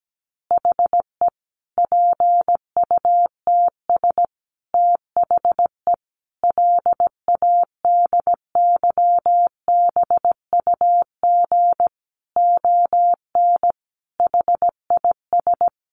Morse code, heard as a keyed tone, about 17 wpm, 700 Hz. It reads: HE PUTS THE LADYBUG ON HIS
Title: Additional Easy Phrases 17wpm